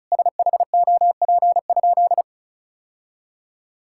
Morse code, heard as a keyed tone, about 36 wpm, 700 Hz. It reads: SHOP?
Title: Sentences from Top 1600 Words 35wpm